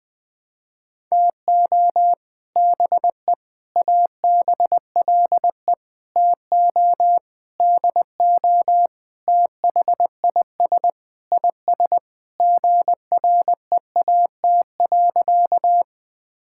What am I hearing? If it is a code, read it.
Morse code, heard as a keyed tone, about 20 wpm, 700 Hz. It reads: TO BE ABLE TO DO THIS IS GREAT.
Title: Sentences with 5-Word Phrases - 20wpm